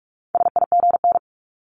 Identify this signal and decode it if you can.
Morse code, heard as a keyed tone, about 46 wpm, 700 Hz. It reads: SIGN
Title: Multi-letter Phonemes 45wpm